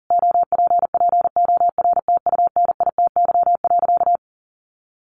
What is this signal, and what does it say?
Morse code, heard as a keyed tone, about 40 wpm, 700 Hz. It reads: OPPORTUNITY.
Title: Sentences from Top 1000 Words 40wpm